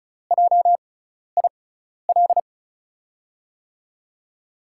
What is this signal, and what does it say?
Morse code, heard as a keyed tone, about 35 wpm, spaced as for 12 wpm, 700 Hz. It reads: JIL
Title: USA Call Signs - 6 Characters - 6x Character Spacing - 35wpm